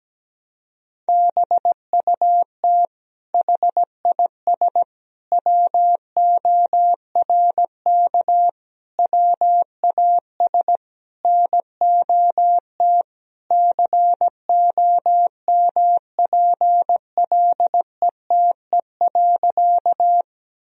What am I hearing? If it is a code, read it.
Morse code, heard as a keyed tone, about 17 wpm, 700 Hz. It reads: BUT HIS WORK WAS NOT COMPLETE.
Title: Sentences from Top 1000 Words - Review Part 2 - 17wpm